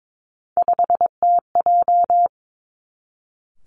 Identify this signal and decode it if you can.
Morse code, heard as a keyed tone, about 22 wpm, 700 Hz. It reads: 5TJ
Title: Three Challenging Letter-Number Combinations 22wpm